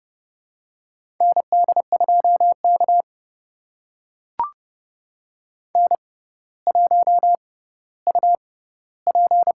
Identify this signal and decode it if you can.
Morse code, heard as a keyed tone, about 30 wpm, spaced as for 15 wpm, 700 Hz. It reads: ND2K N1UP
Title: USA Call Signs - 4 Characters - 6x Character Spacing - 30wpm